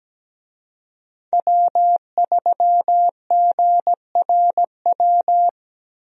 Morse code, heard as a keyed tone, about 17 wpm, 700 Hz. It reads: W3GRW
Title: USA Call Signs - 5 Characters - Repeated 2x - 17wpm